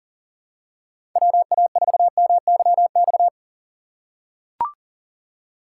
Morse code, heard as a keyed tone, about 40 wpm, 700 Hz. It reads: WA4MYX
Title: USA Call Signs - 6 Characters - 3x Character Spacing - 40wpm